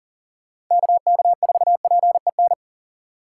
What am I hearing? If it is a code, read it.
Morse code, heard as a keyed tone, about 38 wpm, 700 Hz. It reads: KK4PEN
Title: USA Call Signs - 6 Characters - 4x Character Spacing - 40wpm